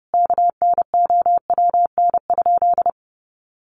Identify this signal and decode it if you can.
Morse code, heard as a keyed tone, about 30 wpm, 700 Hz. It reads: KNOWN?
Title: Sentences from Top 1200 Words 30wpm